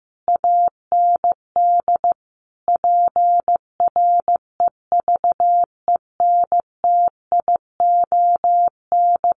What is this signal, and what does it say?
Morse code, heard as a keyed tone, about 15 wpm, 700 Hz. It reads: AND PREVENTION
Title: Sets of 5 Words Encore2 15wpm